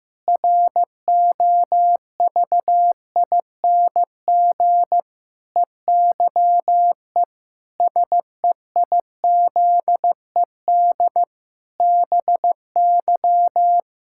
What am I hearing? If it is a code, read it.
Morse code, heard as a keyed tone, about 15 wpm, 700 Hz. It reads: ROVING EYE SEIZED BY
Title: Pangram Challenge v2 - 15wpm